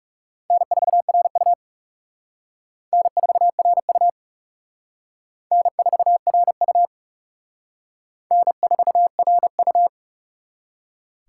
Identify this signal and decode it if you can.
Morse code, heard as a keyed tone, about 38 wpm, 700 Hz. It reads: N4RU N4RU N4RU N4RU